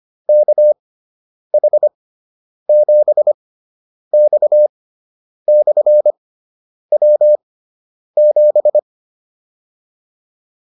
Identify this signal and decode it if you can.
Morse code, heard as a keyed tone, about 25 wpm, spaced as for 9 wpm, 600 Hz. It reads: KH7X/W7